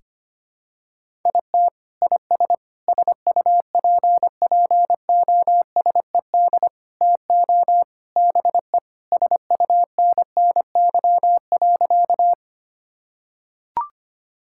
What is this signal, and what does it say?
Morse code, heard as a keyed tone, about 25 wpm, 700 Hz. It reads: IT IS SUPPOSED TO BE SUNNY.
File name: practice / 25wpm / Sentences with 3-Word Phrases - 25wpm